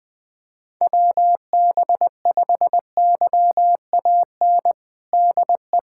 Morse code, heard as a keyed tone, about 21 wpm, 700 Hz. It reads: WB5YAN DE
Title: USA Callsign DE Callsign 20wpm